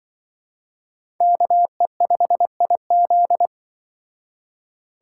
Morse code, heard as a keyed tone, about 23 wpm, 700 Hz. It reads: KE5IZ
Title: Speed Racing - USA 2x2 Call Signs 20wpm